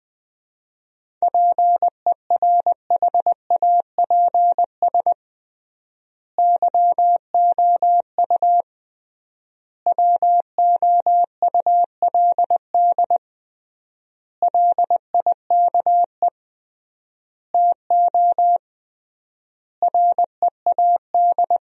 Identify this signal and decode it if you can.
Morse code, heard as a keyed tone, about 20 wpm, 700 Hz. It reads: PERHAPS YOU WOULD LIKE TO READ
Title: 3x Word-Spacing - Sentences from Top 500 Words - 20wpm